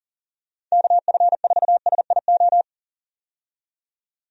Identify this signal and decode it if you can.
Morse code, heard as a keyed tone, about 40 wpm, 700 Hz. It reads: KF4SIO
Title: USA Call Signs - 6 Characters - 40wpm